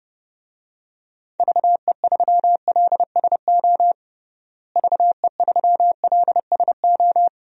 Morse code, heard as a keyed tone, about 30 wpm, 700 Hz. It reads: VE3LSO VE3LSO
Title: DXCC-100 Call Signs - 6 Characters - Repeated 2x - 30wpm